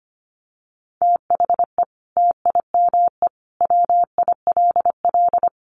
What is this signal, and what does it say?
Morse code, heard as a keyed tone, about 25 wpm, 700 Hz. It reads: THE TIME WILL
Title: Sentences from Top 100 Words 25wpm